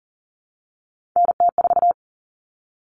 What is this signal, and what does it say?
Morse code, heard as a keyed tone, about 40 wpm, 700 Hz. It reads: NT4
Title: Common Cut Numbers - 3-Digit 40wpm